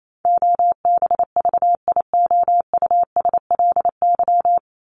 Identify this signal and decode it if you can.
Morse code, heard as a keyed tone, about 28 wpm, 700 Hz. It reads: OBVIOUSLY
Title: Top 1501-1750 Words 28wpm